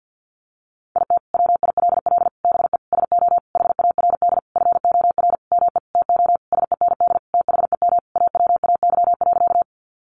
Morse code, heard as a keyed tone, about 50 wpm, 700 Hz. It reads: IT WILL BE SO HARD FOR ME TO SEND THEM AWAY.
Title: Sentences from Top 1500 Words 50wpm